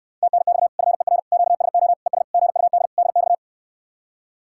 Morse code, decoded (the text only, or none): TNX FER QSO ES GUD DX